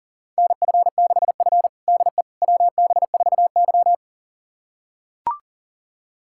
Morse code, heard as a keyed tone, about 40 wpm, 700 Hz. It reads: NF6F DE WB4Y
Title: USA Callsign DE Callsign - 4 Characters - 40wpm